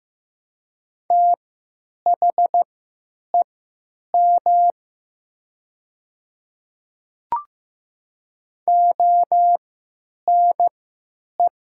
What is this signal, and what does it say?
Morse code, heard as a keyed tone, about 15 wpm, spaced as for 9 wpm, 700 Hz. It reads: THEM ONE
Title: 3x Character Spacing - Top 100 Words - 15wpm